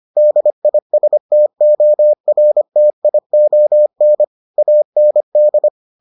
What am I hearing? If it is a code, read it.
Morse code, heard as a keyed tone, about 25 wpm, 600 Hz. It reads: DISTORTION AND